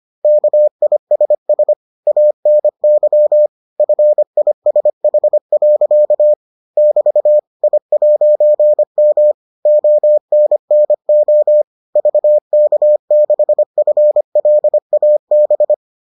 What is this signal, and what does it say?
Morse code, heard as a keyed tone, about 25 wpm, 600 Hz. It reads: KISS ANY FISH. <BT> I'M ONNO VK6FLAB